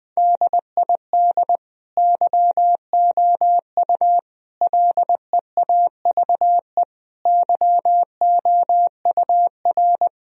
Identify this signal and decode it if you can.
Morse code, heard as a keyed tone, about 20 wpm, 700 Hz. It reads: DID YOU LEAVE YOUR
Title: Sentences from Top 1000 Words - Review Part 3 - 20wpm